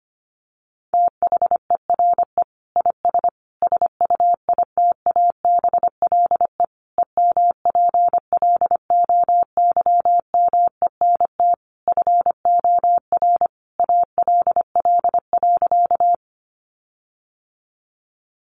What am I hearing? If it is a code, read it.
Morse code, heard as a keyed tone, about 25 wpm, 700 Hz. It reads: THERE IS SUITABLE EMPLOYMENT FOR ALL.
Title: Sentences from Top 1500 Words 25wpm